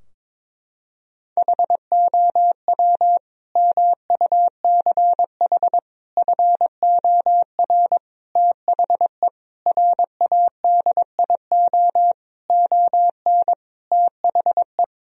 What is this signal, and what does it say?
Morse code, heard as a keyed tone, about 22 wpm, 700 Hz. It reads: HOW MUCH FOR THE RADIO ON THE
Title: Sentences from Top 1000 Words - Review Part 3 - 22wpm